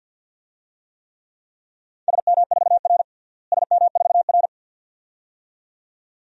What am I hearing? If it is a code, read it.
Morse code, heard as a keyed tone, about 46 wpm, 700 Hz. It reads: SM4R SM4R
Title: DXCC-100 Call Signs - 4 Characters - Repeated 2x - 50wpm